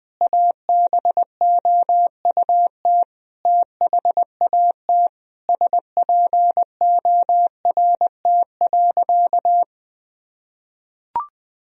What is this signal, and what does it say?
Morse code, heard as a keyed tone, about 20 wpm, 700 Hz. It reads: ABOUT THAT SPORT.
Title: Sentences from Top 1000 Words - Review Part 4 - 20wpm